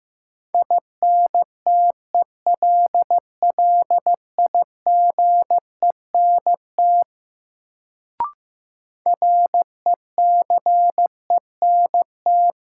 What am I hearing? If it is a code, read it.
Morse code, heard as a keyed tone, about 15 wpm, 700 Hz. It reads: INTELLIGENT RECENT
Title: Sentences from Top 1000 Words - Review Part 7 - New Word Drill - 15wpm